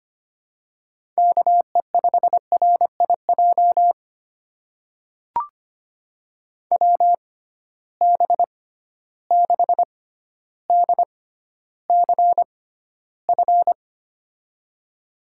Morse code, heard as a keyed tone, about 25 wpm, spaced as for 13 wpm, 700 Hz. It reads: KE5RIJ WB6DCF
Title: USA Call Signs - 6 Characters - 6x Character Spacing - 25wpm